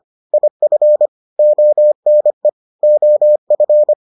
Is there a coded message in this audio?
Yes: Morse code from 0.3 to 3.9 s, heard as a keyed tone, about 25 wpm, 600 Hz. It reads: IF ONE OF